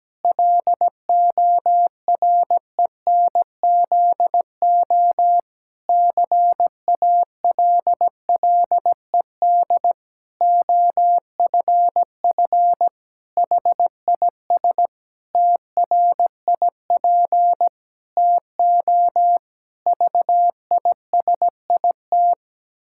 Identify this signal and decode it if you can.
Morse code, heard as a keyed tone, about 17 wpm, 700 Hz. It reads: LORENZO CALLED OFF HIS TRIP TO VISIT
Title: Pangram Challenge v2 - 17wpm